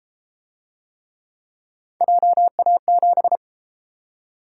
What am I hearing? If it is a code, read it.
Morse code, heard as a keyed tone, about 30 wpm, 700 Hz. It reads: JA7